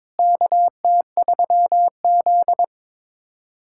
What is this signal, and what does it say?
Morse code, heard as a keyed tone, about 22 wpm, 700 Hz. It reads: KT3Z
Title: USA Callsign DE Callsign - 4 Characters - 22wpm